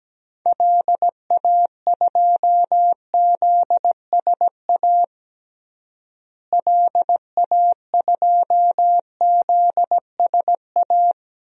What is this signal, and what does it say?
Morse code, heard as a keyed tone, about 17 wpm, 700 Hz. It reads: LA2ZSA LA2ZSA
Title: DXCC-100 Call Signs - 6 Characters - Repeated 2x - 17wpm